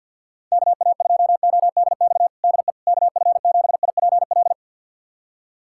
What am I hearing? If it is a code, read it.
Morse code, heard as a keyed tone, about 52 wpm, 700 Hz. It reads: KA2ODX DE KF7IPL
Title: USA Callsign DE Callsign - 6 Characters - 50wpm